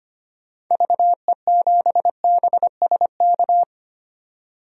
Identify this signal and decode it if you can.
Morse code, heard as a keyed tone, about 25 wpm, 700 Hz. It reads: VE7BSK
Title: DXCC-100 Call Signs - 6 Characters - 25wpm